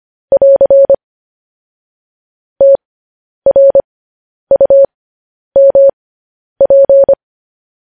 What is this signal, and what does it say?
Morse code, heard as a keyed tone, about 25 wpm, spaced as for 10 wpm, 550 Hz. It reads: <AR> TRUMP
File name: podcast / News headlines-1x-25x10